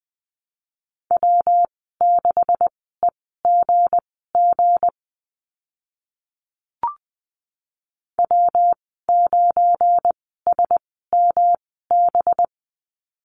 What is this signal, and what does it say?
Morse code, heard as a keyed tone, about 20 wpm, spaced as for 14 wpm, 700 Hz. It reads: W6EGG W9SMB
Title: 193 Lesson 31g - Letter v - Call Signs 20wpm